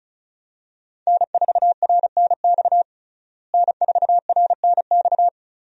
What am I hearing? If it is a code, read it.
Morse code, heard as a keyed tone, about 34 wpm, 700 Hz. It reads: N4RNX N4RNX
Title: USA Call Signs - 5 Characters - Repeated 2x - 35wpm